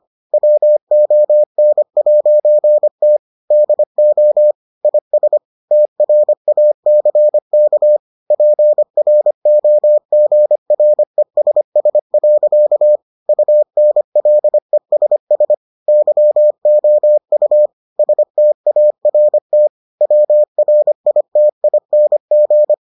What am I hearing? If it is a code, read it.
Morse code, heard as a keyed tone, about 25 wpm, 600 Hz. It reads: WON'T DO IS TRACK PROGRESS. UNLESS YOU START WRITING